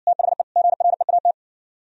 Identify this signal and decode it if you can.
Morse code, heard as a keyed tone, about 65 wpm, 700 Hz. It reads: THE GREAT